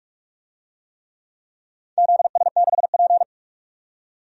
Morse code, heard as a keyed tone, about 44 wpm, 700 Hz. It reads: ZS6P
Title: DXCC-100 Call Signs - 4 Characters - 45wpm